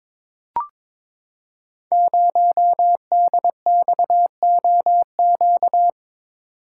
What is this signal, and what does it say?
Morse code, heard as a keyed tone, about 22 wpm, 700 Hz. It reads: 0DXOQ
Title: Code Groups - 5 Letter:Numbers - 22wpm